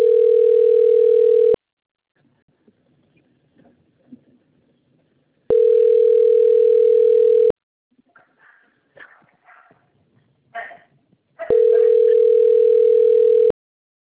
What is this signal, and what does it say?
Ringback tone (North America), 440 + 480 Hz.
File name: rg-1600-unknown- - - .78759